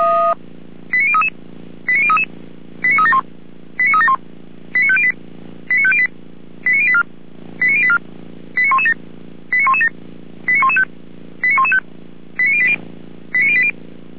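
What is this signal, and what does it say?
Five-tone selective call (ZVEI), heard as quick runs of five tones, tones of about 70 ms. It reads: 89033 89033 89371 89371 89508 89508 89084 89084 89117 89117 89115 89115 89080 89080
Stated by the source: ZVEI five-tone selective call received via SDR.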